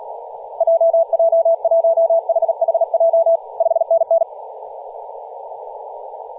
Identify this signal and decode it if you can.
Morse code, heard as a keyed tone, about 38 wpm, 660 Hz, with radio noise. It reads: JJ1HHJ 5NN
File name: posts / 2016-aug-26-2236z-yj0cow-18-073mhz-cw